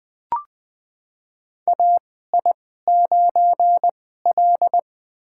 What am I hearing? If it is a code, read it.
Morse code, heard as a keyed tone, about 20 wpm, spaced as for 14 wpm, 700 Hz. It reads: AI9L
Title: USA Call Signs - 4 Characters - 2x Character Spacing - 20wpm